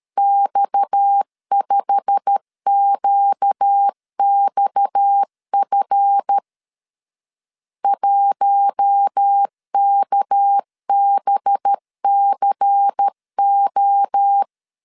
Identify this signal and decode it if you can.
Morse code, heard as a keyed tone, about 13 wpm, 800 Hz. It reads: X5QXF 1KBCO
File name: CW / DET / 1 9 1500C S B 12wpm 3819